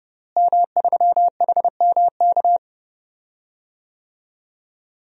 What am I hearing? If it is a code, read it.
Morse code, heard as a keyed tone, about 31 wpm, 700 Hz. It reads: M3HMK
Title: DXCC-100 Call Signs - 5 Characters - Repeated 2x - 30wpm